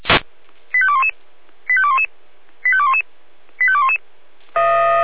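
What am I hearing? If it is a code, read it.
Five-tone selective call (ZVEI), heard as quick runs of five tones, tones of about 70 ms. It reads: 85211 85211 85211 85211
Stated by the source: ZVEI five-tone selective call received by SDR.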